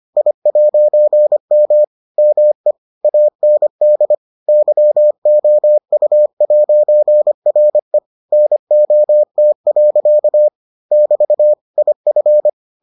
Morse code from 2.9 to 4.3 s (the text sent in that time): AND